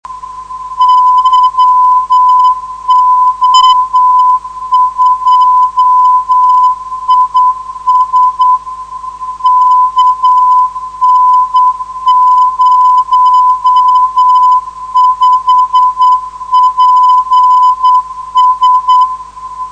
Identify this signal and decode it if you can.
Morse code, heard as a keyed tone, about 9 wpm, 1000 Hz. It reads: YO2IS KN05PS
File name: 472khz web swl / 14dec2012 / yo2is narow